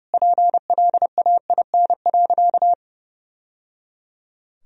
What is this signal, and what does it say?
Morse code, heard as a keyed tone, about 30 wpm, 700 Hz. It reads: PLAIN.
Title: Sentences from Top 1000 Words - Review Part 4 - 30wpm